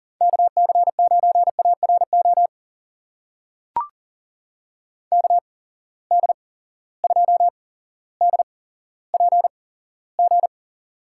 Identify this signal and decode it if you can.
Morse code, heard as a keyed tone, about 40 wpm, spaced as for 17 wpm, 700 Hz. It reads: KC9ARO KD2DPG
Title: USA Call Signs - 6 Characters - 8x Character Spacing - 40wpm